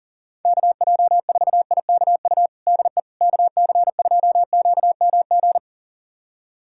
Morse code, heard as a keyed tone, about 40 wpm, 700 Hz. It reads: KJ4IKU DE KC2QMG
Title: USA Callsign DE Callsign - 6 Characters - 40wpm